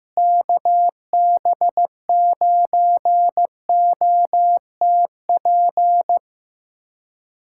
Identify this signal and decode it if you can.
Morse code, heard as a keyed tone, about 15 wpm, 700 Hz. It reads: KB9OTP
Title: USA Callsign DE Callsign - 6 Characters - 15wpm